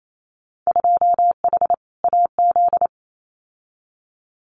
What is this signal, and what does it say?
Morse code, heard as a keyed tone, about 28 wpm, 700 Hz. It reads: 2H AZ
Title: Winter Field Day Exchanges 28wpm